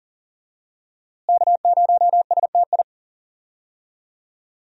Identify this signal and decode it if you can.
Morse code, heard as a keyed tone, about 39 wpm, 700 Hz. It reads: K0STI
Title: USA Call Signs - 5 Characters - Repeated 2x - 40wpm